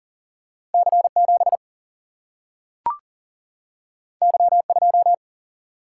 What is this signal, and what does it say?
Morse code, heard as a keyed tone, about 40 wpm, 700 Hz. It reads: C7 Y2
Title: Single-Letter Single-Number 40wpm